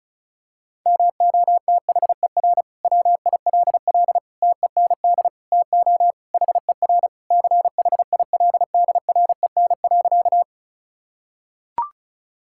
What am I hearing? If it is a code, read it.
Morse code, heard as a keyed tone, about 34 wpm, 700 Hz. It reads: MOTHER WILL TEND TO HER CHILDREN.
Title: Sentences from Top 1000 Words - Review Part 5 - 35wpm